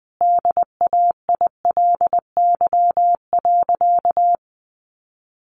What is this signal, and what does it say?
Morse code, heard as a keyed tone, about 20 wpm, 700 Hz. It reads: DAILY.
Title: Sentences from Top 1100 Words 20wpm